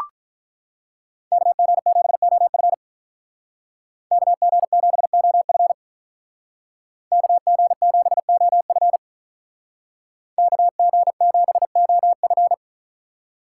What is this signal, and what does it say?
Morse code, heard as a keyed tone, about 44 wpm, 700 Hz. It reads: KG7OF KG7OF KG7OF KG7OF